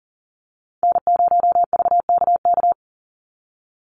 Morse code, heard as a keyed tone, about 40 wpm, 700 Hz. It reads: N0VKK
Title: USA Call Signs 40wpm